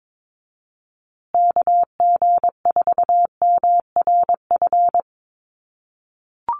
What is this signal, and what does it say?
Morse code, heard as a keyed tone, about 22 wpm, 700 Hz. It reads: KG4MRF
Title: USA Call Signs Repeated 2x 22wpm